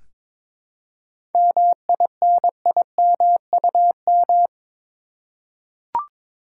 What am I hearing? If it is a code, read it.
Morse code, heard as a keyed tone, about 22 wpm, 700 Hz. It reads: MINIMUM
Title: Seven Letter Words - 22wpm